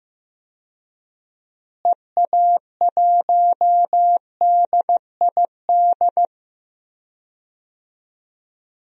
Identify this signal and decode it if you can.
Morse code, heard as a keyed tone, about 15 wpm, 700 Hz. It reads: EA1DID
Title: DXCC-100 Call Signs - 6 Characters - Repeated 2x - 15wpm